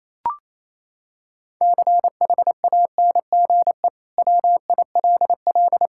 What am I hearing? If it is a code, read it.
Morse code, heard as a keyed tone, about 28 wpm, 700 Hz. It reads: CHANGE WILL
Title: Sentences from Top 1000 Words - Review Part 5 - 28wpm